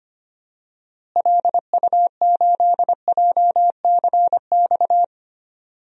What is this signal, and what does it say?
Morse code, heard as a keyed tone, about 25 wpm, 700 Hz. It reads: LU8JCX
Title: DXCC-100 Call Signs - 6 Characters - Repeated 3x - 25wpm